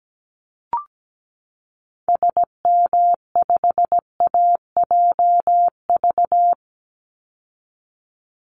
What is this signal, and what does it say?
Morse code, heard as a keyed tone, about 17 wpm, 700 Hz. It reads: SM5AJV
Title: Call Signs Repeated 3x - 17wpm